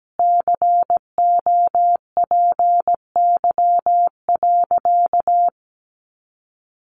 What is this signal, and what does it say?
Morse code, heard as a keyed tone, about 17 wpm, 700 Hz. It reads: COPY.
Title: Sentences from Top 1400 Words 17wpm